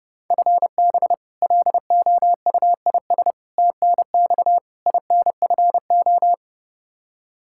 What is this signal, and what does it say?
Morse code, heard as a keyed tone, about 32 wpm, 700 Hz. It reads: FB LOUIS TNX INFO
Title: K1USN SST QSO Practice 30wpm